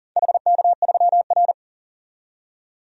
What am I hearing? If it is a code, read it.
Morse code, heard as a keyed tone, about 40 wpm, 700 Hz. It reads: HK3R
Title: DXCC-100 Call Signs - 4 Characters - 40wpm